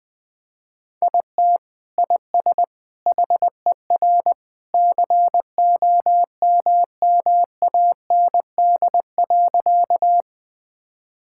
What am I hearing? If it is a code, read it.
Morse code, heard as a keyed tone, about 21 wpm, 700 Hz. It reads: IT IS HER COMMAND.
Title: Sentences from Top 1600 Words 20wpm